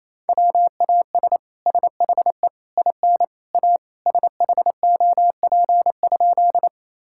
Morse code, heard as a keyed tone, about 27 wpm, 700 Hz. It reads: WAS SHE IN A SHOP?
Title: Sentences from Top 1600 Words 28wpm